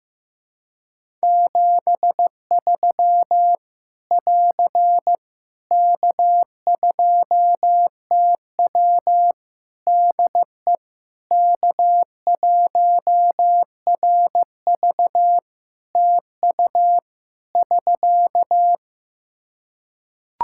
Morse code, heard as a keyed tone, about 16 wpm, 700 Hz. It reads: 73 <AR> K2TW DE K1RV TU <SK>
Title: K1USN SST QSO Practice 15wpm